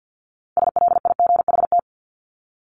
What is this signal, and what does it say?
Morse code, heard as a keyed tone, about 50 wpm, 700 Hz. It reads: SLIGHT
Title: LY Suffix - Top 100 Words with Base Words 50wpm